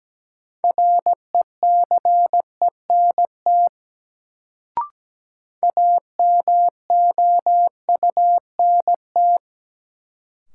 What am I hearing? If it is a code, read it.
Morse code, heard as a keyed tone, about 17 wpm, 700 Hz. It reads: RECENT AMOUNT
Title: Sets of 3 Words - New Word Drill - 17wpm